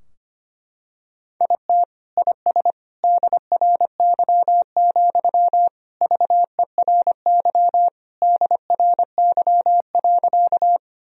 Morse code, heard as a keyed tone, about 25 wpm, 700 Hz. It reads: IT IS DRY, VERY DRY.
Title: Sentences from Top 1000 Words - Review Part 3 - 25wpm